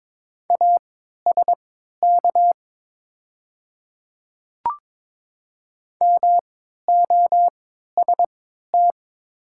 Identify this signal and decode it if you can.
Morse code, heard as a keyed tone, about 22 wpm, spaced as for 12 wpm, 700 Hz. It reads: ASK MOST
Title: 3x Character Spacing - Top 200 Words - 22wpm